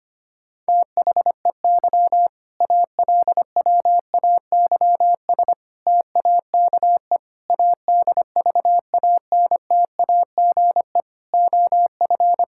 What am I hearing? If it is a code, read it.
Morse code, heard as a keyed tone, about 25 wpm, 700 Hz. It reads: THEY ALWAYS TAKE ADVANTAGE OF
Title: Sentences from Top 1000 Words - Review Part 6 - 25wpm